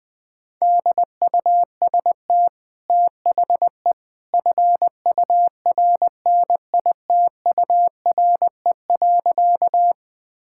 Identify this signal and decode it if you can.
Morse code, heard as a keyed tone, about 20 wpm, 700 Hz. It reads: DUST THE FURNITURE.
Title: Easy Sentences with Hints - 20wpm